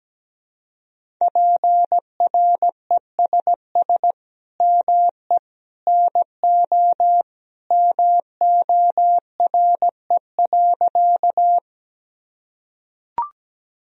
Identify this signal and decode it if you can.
Morse code, heard as a keyed tone, about 17 wpm, 700 Hz. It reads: PRESS ME NO MORE.
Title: Sentences from Top 1000 Words - Review Part 1 - 17wpm